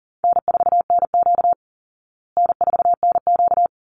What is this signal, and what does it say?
Morse code, heard as a keyed tone, about 40 wpm, 700 Hz. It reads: N4NQ N4NQ
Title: Easy Call Signs Repeated 3x - 40wpm